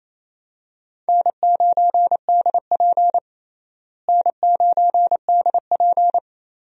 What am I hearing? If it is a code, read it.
Morse code, heard as a keyed tone, about 28 wpm, 700 Hz. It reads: N9DP N9DP
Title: USA Call Signs - 4 Characters - Repeated 3x - 28wpm